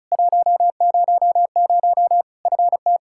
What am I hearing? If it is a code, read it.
Morse code, heard as a keyed tone, about 36 wpm, 700 Hz. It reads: 100 FT
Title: Practice QSO - Exchange 2 - Info - 35wpm